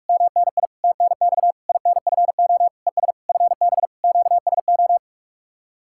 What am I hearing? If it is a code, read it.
Morse code, heard as a keyed tone, about 65 wpm, spaced as for 55 wpm, 700 Hz. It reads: MNI TNX INFO ES FB QSO
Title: Speed-Racing Common QSO Phrases - 45wpm